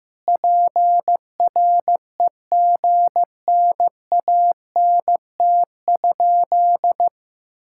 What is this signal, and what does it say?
Morse code, heard as a keyed tone, about 15 wpm, 700 Hz. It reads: PREGNANT?
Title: Sentences from Top 1000 Words - Review Part 6 - 15wpm